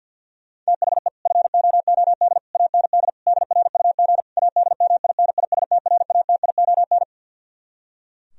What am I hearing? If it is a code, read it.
Morse code, heard as a keyed tone, about 49 wpm, 700 Hz. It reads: THE FOOD AND DRUG ADMINISTRATION
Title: Speed-Racing - Sets of 5 Words 50wpm